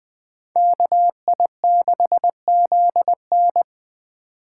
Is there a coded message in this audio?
Yes: Morse code from 0.6 to 3.6 s, heard as a keyed tone, about 20 wpm, 700 Hz. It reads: KI6ZN